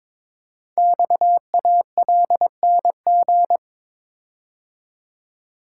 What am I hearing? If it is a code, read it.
Morse code, heard as a keyed tone, about 22 wpm, 700 Hz. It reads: XALNG
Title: Code Groups - 5 Letters - 22wpm